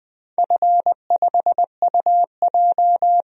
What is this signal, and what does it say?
Morse code, heard as a keyed tone, about 20 wpm, 700 Hz. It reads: F5UJ
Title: DXCC-100 Call Signs - 4 Characters - Repeated 2x - 20wpm